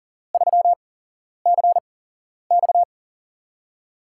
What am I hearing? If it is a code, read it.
Morse code, heard as a keyed tone, about 40 wpm, spaced as for 11 wpm, 700 Hz. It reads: 3CX
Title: USA Call Signs - 5 Characters - 8x Character Spacing - 40wpm